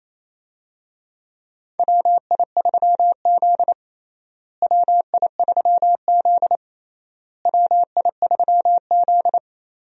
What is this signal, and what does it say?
Morse code, heard as a keyed tone, about 28 wpm, 700 Hz. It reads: WI3Z WI3Z WI3Z
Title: USA Call Signs - 4 Characters - Repeated 3x - 28wpm